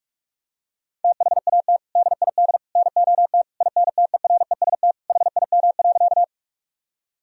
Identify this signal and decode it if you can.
Morse code, heard as a keyed tone, about 46 wpm, 700 Hz. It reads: THAT DID NOT INTEREST HIM.
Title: Sentences from Top 1000 Words - Review Part 2 - 45wpm